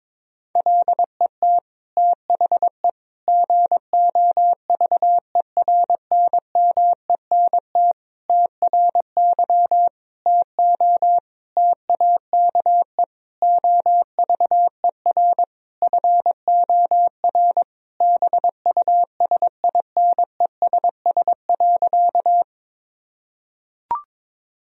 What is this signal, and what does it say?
Morse code, heard as a keyed tone, about 22 wpm, 700 Hz. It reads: LET THE GOVERNMENT TRY TO TAKE OVER FOR BUSINESS.
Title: Sentences from Top 1000 Words - Review Part 3 - 22wpm